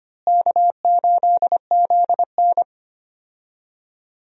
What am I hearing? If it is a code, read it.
Morse code, heard as a keyed tone, about 25 wpm, 700 Hz. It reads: K8ZN
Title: USA Call Signs - 4 Characters - Repeated 3x - 25wpm